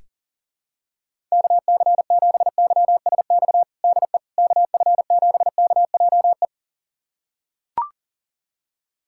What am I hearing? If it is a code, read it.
Morse code, heard as a keyed tone, about 40 wpm, 700 Hz. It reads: KC7YSX DE KF7KJE
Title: USA Callsign DE Callsign - 6 Characters - 40wpm